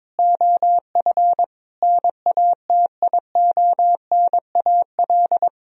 Morse code, heard as a keyed tone, about 22 wpm, 700 Hz. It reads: OF NATIONAL
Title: Sentences from Top 1000 Words - Review Part 4 - 22wpm